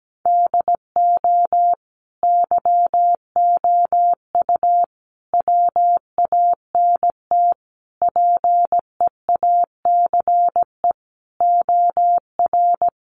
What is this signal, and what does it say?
Morse code, heard as a keyed tone, about 17 wpm, 700 Hz. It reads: DO YOU WANT PEACE OR
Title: Sentences from Top 1500 Words 17wpm